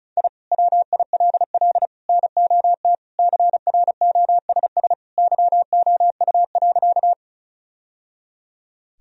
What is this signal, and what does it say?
Morse code, heard as a keyed tone, about 36 wpm, 700 Hz. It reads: I WILL NOT CROSS YOU.
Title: Sentences from Top 1000 Words - Review Part 1 - 35wpm